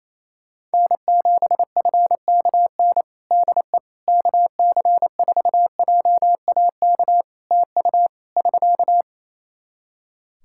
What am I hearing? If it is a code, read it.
Morse code, heard as a keyed tone, about 28 wpm, 700 Hz. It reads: N7FKN DE KC4JAK TU <SK>
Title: Practice QSO - Exchange 4 - Ending - 28wpm